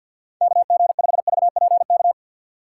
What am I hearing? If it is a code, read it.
Morse code, heard as a keyed tone, about 52 wpm, 700 Hz. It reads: KG5VPK
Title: USA Call Signs - 6 Characters - 6x Character Spacing - 50wpm